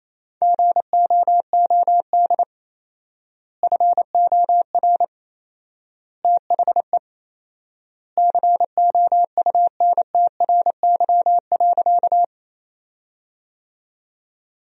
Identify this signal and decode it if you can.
Morse code, heard as a keyed tone, about 28 wpm, 700 Hz. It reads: GOOD FOR THE COUNTRY.
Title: 4x Word-Spacing - Sentences from Top 300 Words - 28wpm